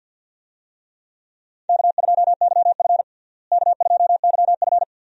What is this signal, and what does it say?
Morse code, heard as a keyed tone, about 52 wpm, 700 Hz. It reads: K2YF K2YF
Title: USA Call Signs - 4 Characters - Repeated 2x - 50wpm